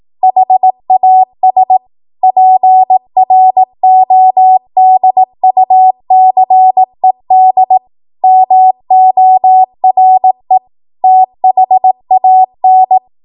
Morse code, heard as a keyed tone, about 18 wpm, 750 Hz. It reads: HAS PRODUCED MORE THAN
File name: Morse / 18WPM